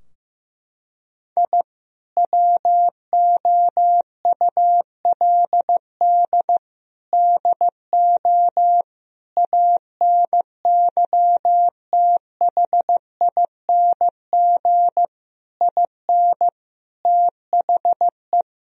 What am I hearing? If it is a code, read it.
Morse code, heard as a keyed tone, about 15 wpm, 700 Hz. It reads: I WOULD DO ANYTHING IN THE
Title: Sentences from Top 1000 Words - Review Part 2 - 15wpm